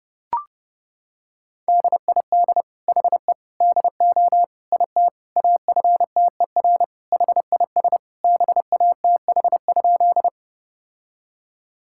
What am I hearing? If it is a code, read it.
Morse code, heard as a keyed tone, about 30 wpm, 700 Hz. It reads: DID HE DO IT AFTER HIS BATH?
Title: Sentences from Top 1600 Words 30wpm